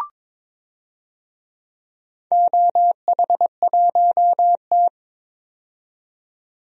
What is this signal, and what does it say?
Morse code, heard as a keyed tone, about 22 wpm, 700 Hz. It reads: OH1T
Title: DXCC-100 Call Signs - 4 Characters - 22wpm